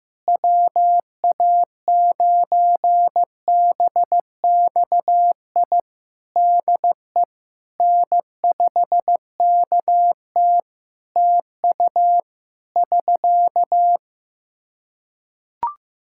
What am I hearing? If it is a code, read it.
Morse code, heard as a keyed tone, about 15 wpm, 700 Hz. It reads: WA9BXI DE N5KT TU <SK>
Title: Practice QSO - Exchange 4 - Ending - 15wpm